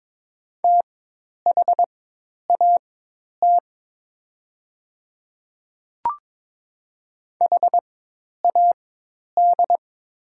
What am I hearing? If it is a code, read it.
Morse code, heard as a keyed tone, about 22 wpm, spaced as for 10 wpm, 700 Hz. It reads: THAT HAD
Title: 4x Character Spacing - Top 100 Words - 22wpm